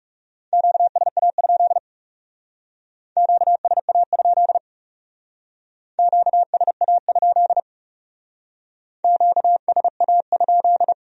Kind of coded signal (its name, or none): Morse code